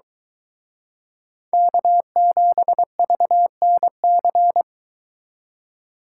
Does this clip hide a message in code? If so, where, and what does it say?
Yes: Morse code from 1.5 to 4.6 s, heard as a keyed tone, about 22 wpm, 700 Hz. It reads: K7VNC